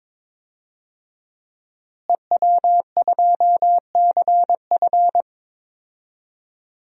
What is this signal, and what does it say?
Morse code, heard as a keyed tone, about 22 wpm, 700 Hz. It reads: EW2CF
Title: DXCC-100 Call Signs - 5 Characters - Repeated 3x - 22wpm